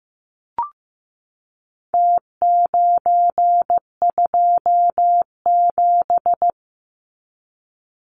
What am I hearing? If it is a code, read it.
Morse code, heard as a keyed tone, about 15 wpm, 700 Hz. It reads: T927
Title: Numbers - Contest Style - 4-Digit 15wpm